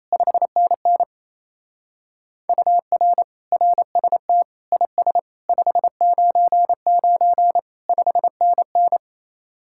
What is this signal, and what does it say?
Morse code, heard as a keyed tone, about 34 wpm, 700 Hz. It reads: 5NN UR RST IS 599 5NN
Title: Speed-Racing Common QSO Phrases - 28wpm